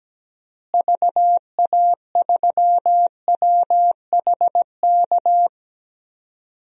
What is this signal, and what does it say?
Morse code, heard as a keyed tone, about 17 wpm, 700 Hz. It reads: VA3WHK
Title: DXCC-100 Call Signs - 6 Characters - Repeated 2x - 17wpm